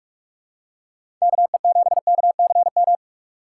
Morse code, heard as a keyed tone, about 42 wpm, 700 Hz. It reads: KE7KCM
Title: USA Call Signs - 6 Characters - 2x Character Spacing - 45wpm